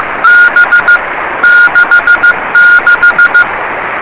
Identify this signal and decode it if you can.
Morse code, heard as a keyed tone, about 15 wpm, 1430 Hz, with noise. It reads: B 66
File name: rs30 cw